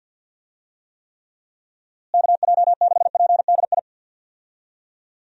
Morse code, heard as a keyed tone, about 50 wpm, 700 Hz. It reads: KJ6PDI
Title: USA Call Signs - 6 Characters - Repeated 3x - 50wpm